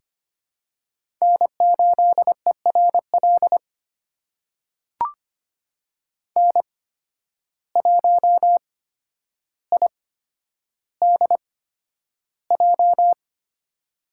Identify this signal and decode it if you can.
Morse code, heard as a keyed tone, about 25 wpm, spaced as for 11 wpm, 700 Hz. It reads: N8ERL N1IDJ
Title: USA Call Signs - 8x Character Spacing - 25wpm